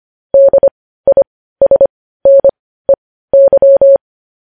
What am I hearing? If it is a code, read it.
Morse code, heard as a keyed tone, about 25 wpm, spaced as for 15 wpm, 550 Hz. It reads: DISNEY
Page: news-headlines-in-morse-code-at-15-wpm